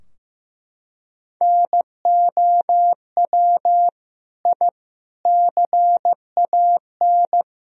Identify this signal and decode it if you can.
Morse code, heard as a keyed tone, about 15 wpm, 700 Hz. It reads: NOW I CAN
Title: Sentences from Top 1000 Words - Review Part 3 - 15wpm